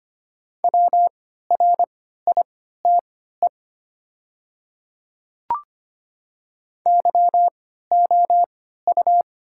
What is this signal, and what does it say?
Morse code, heard as a keyed tone, about 25 wpm, spaced as for 14 wpm, 700 Hz. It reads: WRITE YOU
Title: 3x Character Spacing - Top 300 Words - 25wpm